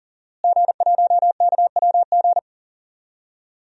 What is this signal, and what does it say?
Morse code, heard as a keyed tone, about 40 wpm, 700 Hz. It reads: G1KWG
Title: DXCC-100 Call Signs - 5 Characters - Repeated 2x - 40wpm